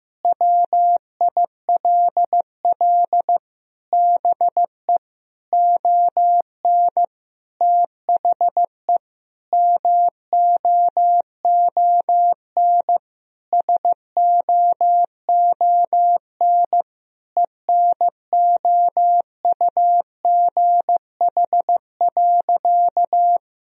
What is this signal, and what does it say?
Morse code, heard as a keyed tone, about 15 wpm, 700 Hz. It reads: WILL BE ON THE MOON SOON ENOUGH.
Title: Sentences from Top 1000 Words - Review Part 3 - 15wpm